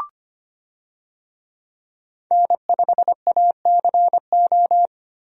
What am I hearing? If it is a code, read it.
Morse code, heard as a keyed tone, about 25 wpm, 700 Hz. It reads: N5ACO
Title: USA Call Signs - 5 Characters - Repeated 3x - 25wpm